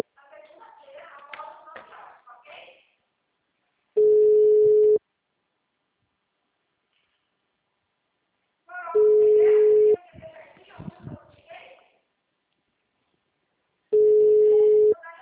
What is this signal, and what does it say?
Ringback tone (Europe, 425 Hz), with speech.